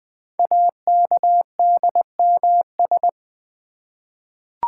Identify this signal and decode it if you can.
Morse code, heard as a keyed tone, about 20 wpm, 700 Hz. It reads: AKDMS
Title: Code Groups - 5 Letters - 20wpm